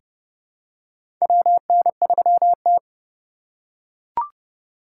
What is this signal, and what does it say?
Morse code, heard as a keyed tone, about 31 wpm, 700 Hz. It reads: WN3T
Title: USA Call Signs - 4 Characters - 4x Character Spacing - 30wpm